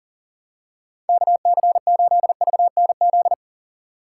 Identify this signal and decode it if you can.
Morse code, heard as a keyed tone, about 40 wpm, 700 Hz. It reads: KC8VNZ
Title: USA Call Signs - 3x Character Spacing - 40wpm